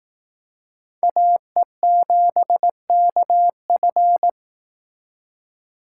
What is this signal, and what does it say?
Morse code, heard as a keyed tone, about 18 wpm, 700 Hz. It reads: AE7KF
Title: Speed Racing - USA 2x2 Call Signs 15wpm